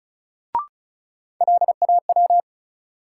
Morse code, heard as a keyed tone, about 36 wpm, 700 Hz. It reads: LAW
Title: Sentences from Top 600 Words - New Word Drill - 35wpm